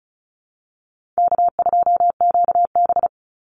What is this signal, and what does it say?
Morse code, heard as a keyed tone, about 34 wpm, 700 Hz. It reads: K2QB
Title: Call Signs Repeated 3x - 35wpm